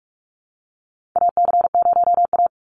Morse code, heard as a keyed tone, about 44 wpm, 700 Hz. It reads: AC0A
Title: NA SOTA Chaser Callsigns 45wpm